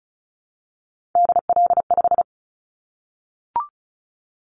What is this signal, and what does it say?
Morse code, heard as a keyed tone, about 36 wpm, 690 Hz. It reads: DL5
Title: Callsign Prefixes for Top 25 DXCC - 35wpm